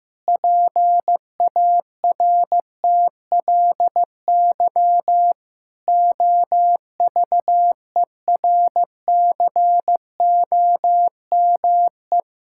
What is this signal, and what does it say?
Morse code, heard as a keyed tone, about 15 wpm, 700 Hz. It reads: PARTLY OVERCOME
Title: Pangram Challenge v2 - 15wpm